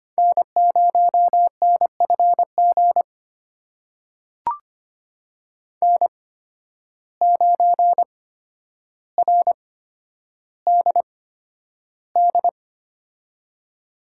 Morse code, heard as a keyed tone, about 25 wpm, spaced as for 11 wpm, 700 Hz. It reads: N0NFG N9RDD
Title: USA Call Signs - 5 Characters - 8x Character Spacing - 25wpm